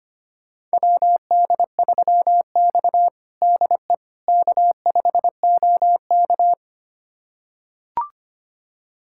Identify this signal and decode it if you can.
Morse code, heard as a keyed tone, about 25 wpm, 700 Hz. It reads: WD3X DE K5OK
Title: USA Callsign DE Callsign - 4 Characters - 25wpm